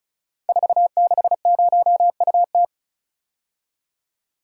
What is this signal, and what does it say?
Morse code, heard as a keyed tone, about 36 wpm, 690 Hz. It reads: 460UT
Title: Code Groups - 5 Letter:Numbers - 35wpm